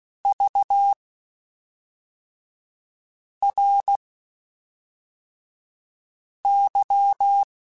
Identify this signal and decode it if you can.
Morse code, heard as a keyed tone, about 16 wpm, spaced as for 3 wpm, 800 Hz. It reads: VRY
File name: (spaced) / QSOs Spaced-mp3s / kc7sac @5 sp